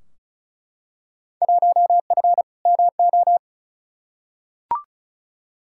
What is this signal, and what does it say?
Morse code, heard as a keyed tone, about 36 wpm, 700 Hz. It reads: 1F MO
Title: Field Day Exchanges Encore - 35wpm